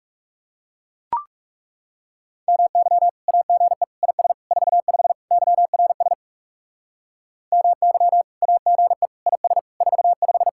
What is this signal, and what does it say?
Morse code, heard as a keyed tone, about 36 wpm, 700 Hz. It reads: MY AGE IS 45 YRS MY AGE IS 45
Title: Speed-Racing Common QSO Phrases Encore - 30wpm